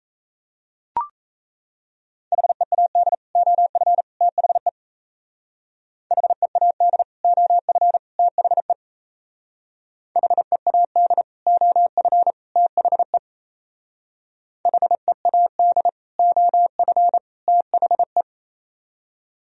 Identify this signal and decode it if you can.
Morse code, heard as a keyed tone, about 35 wpm, 700 Hz. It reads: HEAD OF THE HEAD OF THE HEAD OF THE HEAD OF THE
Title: Speed-Racing - Sets of 3 Words 28wpm